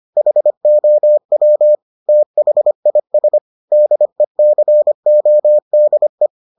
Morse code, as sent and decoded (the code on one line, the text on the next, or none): .... --- .-- / - .... .. ... / -.. -.-. --- -.. .
HOW THIS DECODE